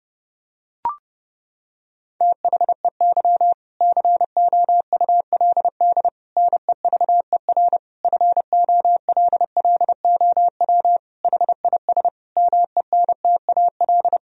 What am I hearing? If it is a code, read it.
Morse code, heard as a keyed tone, about 32 wpm, 700 Hz. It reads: THEY COULD NEVER FOLLOW HIS MENTAL
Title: Sentences from Top 1000 Words - Review Part 4 - 30wpm